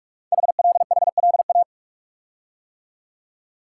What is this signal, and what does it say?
Morse code, heard as a keyed tone, about 44 wpm, 700 Hz. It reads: HLHLA
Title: Code Groups - 5 Letters - 45wpm